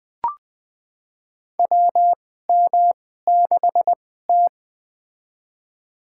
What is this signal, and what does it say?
Morse code, heard as a keyed tone, about 20 wpm, spaced as for 14 wpm, 700 Hz. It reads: WM6T
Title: USA Call Signs - 4 Characters - 2x Character Spacing - 20wpm